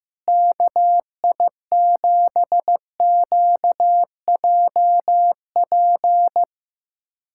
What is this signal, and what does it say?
Morse code, heard as a keyed tone, about 15 wpm, 700 Hz. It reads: KI7QJP
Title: USA Callsign DE Callsign - 6 Characters - 15wpm